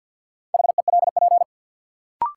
Morse code, heard as a keyed tone, about 53 wpm, 700 Hz. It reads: HELP
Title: Sentences from Top 200 Words - New Word Drill - 50wpm